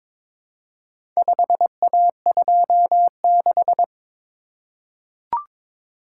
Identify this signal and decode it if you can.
Morse code, heard as a keyed tone, about 22 wpm, 700 Hz. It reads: 5A26
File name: practice / 22wpm / Four Letter-Number Combinations - 22wpm